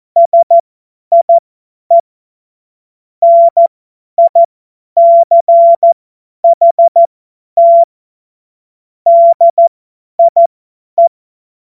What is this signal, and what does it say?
Morse code, heard as a keyed tone, about 14 wpm, spaced as for 10 wpm, 680 Hz. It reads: SIE NICHT DIE
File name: Morsekurs / Witze / Witz 10. Mai 2022 50 F70